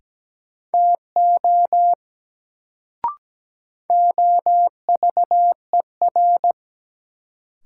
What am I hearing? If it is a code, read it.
Morse code, heard as a keyed tone, about 17 wpm, 700 Hz. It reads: TO OVER
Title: Sentences from Top 100 Words - New Word Drill - 17wpm